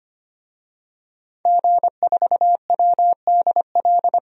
Morse code, heard as a keyed tone, about 25 wpm, 700 Hz. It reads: G4WDL
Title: DXCC-100 Call Signs - 5 Characters - 25wpm